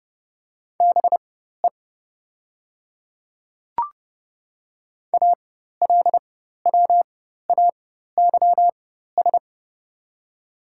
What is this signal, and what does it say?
Morse code, heard as a keyed tone, about 30 wpm, spaced as for 14 wpm, 700 Hz. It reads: BE ALWAYS
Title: 4x Character Spacing - Top 300 Words - 30wpm